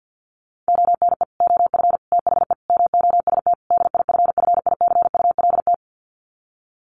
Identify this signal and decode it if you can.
Morse code, heard as a keyed tone, about 50 wpm, 700 Hz. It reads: ONE OF THE MOST DIFFICULT
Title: Sets of 5 Words Encore2 50wpm